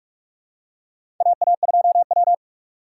Morse code, heard as a keyed tone, about 44 wpm, 700 Hz. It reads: AA2W
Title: USA Call Signs - 2x Character Spacing - 45wpm